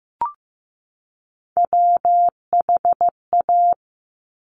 Morse code, heard as a keyed tone, about 15 wpm, 700 Hz. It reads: WHA
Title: Top Three Letter Combinations in Words 15wpm